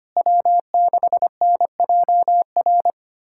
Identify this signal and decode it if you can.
Morse code, heard as a keyed tone, about 25 wpm, 700 Hz. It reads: W6NJR
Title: USA Callsign DE Callsign - 5 Characters - 25wpm